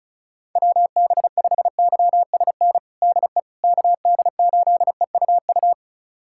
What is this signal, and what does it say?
Morse code, heard as a keyed tone, about 34 wpm, 700 Hz. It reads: WB5YSN DE KD8EUU
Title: USA Callsign DE Callsign - 6 Characters - 35wpm